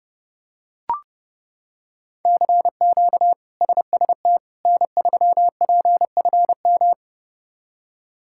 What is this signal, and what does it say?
Morse code, heard as a keyed tone, about 30 wpm, 700 Hz. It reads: CQ SST N3PFM
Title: K1USN SST QSO Practice - 30wpm